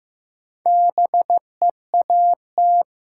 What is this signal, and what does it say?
Morse code, heard as a keyed tone, about 15 wpm, 700 Hz. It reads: BEAT
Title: Sentences from Top 1300 Words - New Word Drill Alt - 15wpm